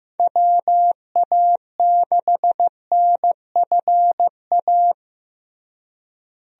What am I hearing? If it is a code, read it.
Morse code, heard as a keyed tone, about 15 wpm, 700 Hz. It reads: WA6NFA
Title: USA Call Signs - 6 Characters - Repeated 2x - 15wpm